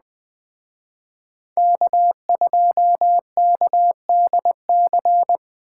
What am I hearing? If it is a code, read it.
Morse code, heard as a keyed tone, about 20 wpm, 700 Hz. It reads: K2KDC
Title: Speed Racing - USA 1x3 Call Signs 17wpm